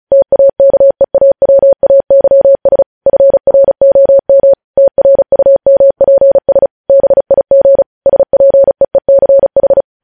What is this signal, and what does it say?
Morse code, heard as a keyed tone, about 36 wpm, 550 Hz. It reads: TAKEAWAYS FROM TRUMPS BIG SPEECH
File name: podcast / News headlines-1x-35x35